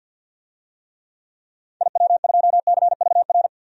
Morse code, heard as a keyed tone, about 50 wpm, 700 Hz. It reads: IW2CVR
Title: DXCC-100 Call Signs - 6 Characters - 50wpm